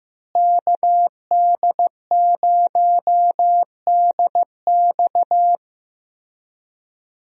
Morse code, heard as a keyed tone, about 15 wpm, 700 Hz. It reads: KD0DX
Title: USA Call Signs - 5 Characters - Repeated 2x - 15wpm